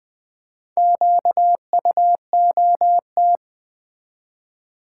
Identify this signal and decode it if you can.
Morse code, heard as a keyed tone, about 20 wpm, 700 Hz. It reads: QUOT
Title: Build A Word - 1 Letter At A Time - 5-Letter Words - 20wpm